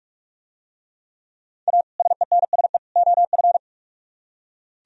Morse code, heard as a keyed tone, about 46 wpm, 700 Hz. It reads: A SENSE OF